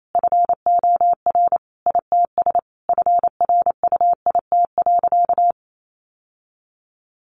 Morse code, heard as a keyed tone, about 28 wpm, 700 Hz. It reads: FOR ITS FRUIT.
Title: Sentences from Top 1300 Words 28wpm